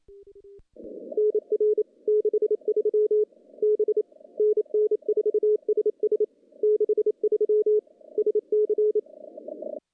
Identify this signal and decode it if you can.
Morse code, heard as a keyed tone, about 27 wpm, 420 Hz. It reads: NR 63 B NN4SS 63 SC
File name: contests / CW-SS-k5tr- -1300